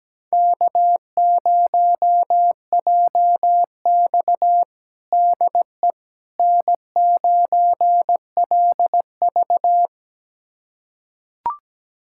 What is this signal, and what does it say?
Morse code, heard as a keyed tone, about 17 wpm, 700 Hz. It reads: K0JX DE N9LV
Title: USA Callsign DE Callsign - 4 Characters - 17wpm